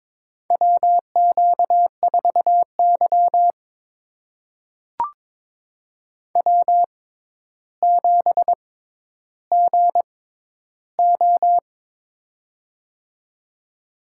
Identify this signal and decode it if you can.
Morse code, heard as a keyed tone, about 21 wpm, spaced as for 11 wpm, 700 Hz. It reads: WQ4Y W7GO
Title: USA Call Signs - 4 Characters - 6x Character Spacing - 22wpm